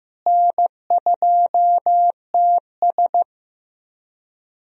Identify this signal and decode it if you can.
Morse code, heard as a keyed tone, about 15 wpm, 700 Hz. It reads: N2TS
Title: USA Call Signs - 4 Characters - Repeated 3x - 15wpm